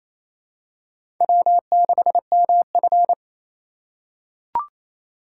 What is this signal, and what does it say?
Morse code, heard as a keyed tone, about 28 wpm, 700 Hz. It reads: W6MF
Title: USA Call Signs - 4 Characters - 2x Character Spacing - 28wpm